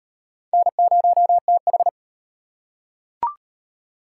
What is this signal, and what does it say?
Morse code, heard as a keyed tone, about 34 wpm, 700 Hz. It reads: N0TH